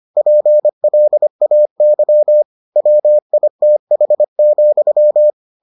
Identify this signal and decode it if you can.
Morse code, heard as a keyed tone, about 25 wpm, 600 Hz. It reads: PLAY WITH,